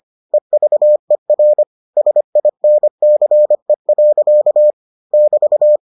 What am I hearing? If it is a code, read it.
Morse code, heard as a keyed tone, about 25 wpm, 600 Hz. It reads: EVER SINCE. <BT>